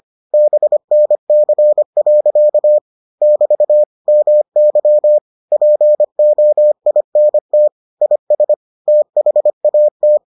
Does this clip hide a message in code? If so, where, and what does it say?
Yes: Morse code from 0.3 to 10.2 s, heard as a keyed tone, about 25 wpm, 600 Hz. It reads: BNC. <BT> MY POINT IS THAT